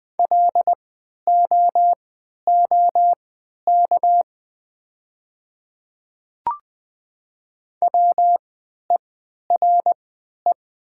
Morse code, heard as a keyed tone, about 20 wpm, spaced as for 11 wpm, 700 Hz. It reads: LOOK WERE
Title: 3x Character Spacing - Top 200 Words - 20wpm